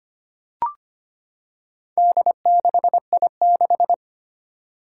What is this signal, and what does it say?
Morse code, heard as a keyed tone, about 25 wpm, 700 Hz. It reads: D6I6
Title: Four Challenging Letter-Number Combinations 25wpm